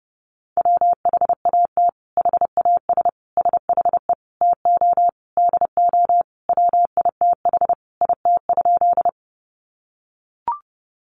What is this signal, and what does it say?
Morse code, heard as a keyed tone, about 30 wpm, 700 Hz. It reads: WHAT HAS SHE TO DO WITH IT?
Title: Sentences from Top 100 Words 30wpm